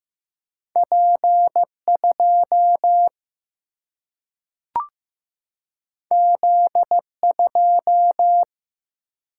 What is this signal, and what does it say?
Morse code, heard as a keyed tone, about 15 wpm, 700 Hz. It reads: P 2 Z 2
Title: Single-Letter Single-Number 15wpm